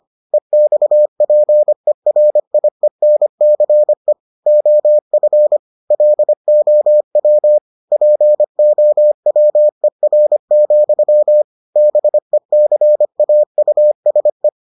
Morse code, heard as a keyed tone, about 25 wpm, 600 Hz. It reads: EXPERIENCE OF LOW POWER, BECAUSE